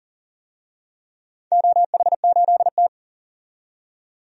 Morse code, heard as a keyed tone, about 38 wpm, 700 Hz. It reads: OH8T
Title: DXCC-100 Call Signs - 4 Characters - 40wpm